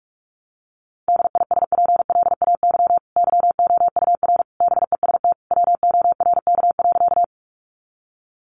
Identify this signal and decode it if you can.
Morse code, heard as a keyed tone, about 44 wpm, 700 Hz. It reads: DISPLAY YOUR BEST WORK.
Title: Sentences from Top 1200 Words 45wpm